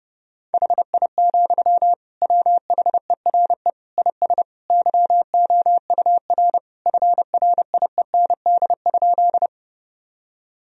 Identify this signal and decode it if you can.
Morse code, heard as a keyed tone, about 30 wpm, 700 Hz. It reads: HI, WHERE IS YOUR FRIEND?
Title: Sentences from Top 1000 Words - Review Part 3 - 30wpm